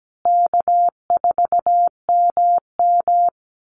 Morse code, heard as a keyed tone, about 17 wpm, 700 Hz. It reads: K4MM
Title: Easy Call Signs Repeated 2x - 17wpm